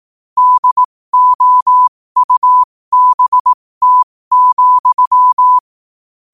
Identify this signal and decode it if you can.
Morse code, heard as a keyed tone, about 18 wpm, spaced as for 15 wpm, 1000 Hz. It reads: DOUBT,
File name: qotdfiles / 15wpm